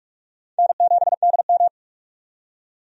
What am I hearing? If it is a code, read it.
Morse code, heard as a keyed tone, about 46 wpm, 700 Hz. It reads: N7DM
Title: USA Call Signs - 4 Characters - 45wpm